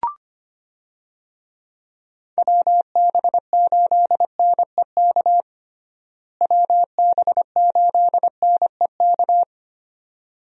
Morse code, heard as a keyed tone, about 25 wpm, 700 Hz. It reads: WB8NEK WB8NEK
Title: USA Call Signs - 6 Characters - Repeated 2x - 25wpm